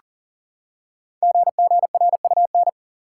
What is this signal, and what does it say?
Morse code, heard as a keyed tone, about 39 wpm, 690 Hz. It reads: GGRUN
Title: Code Groups - 5 Letters - 40wpm